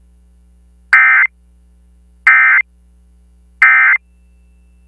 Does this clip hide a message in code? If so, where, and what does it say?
Yes: Emergency Alert System (SAME) tones from 0.9 to 4.0 s, heard as harsh buzzing bursts.